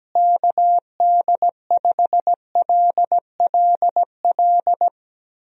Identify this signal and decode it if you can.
Morse code, heard as a keyed tone, about 17 wpm, 700 Hz. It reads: KD5LLL
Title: USA Call Signs - 6 Characters - Repeated 3x - 17wpm